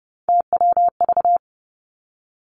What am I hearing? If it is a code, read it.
Morse code, heard as a keyed tone, about 30 wpm, 700 Hz. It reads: TWV
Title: Three Challenging Letter-Number Combinations 30wpm